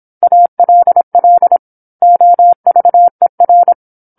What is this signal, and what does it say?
Morse code, heard as a keyed tone, about 27 wpm, 700 Hz. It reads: ALL OVER
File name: 08-Chapter-08 26 WPM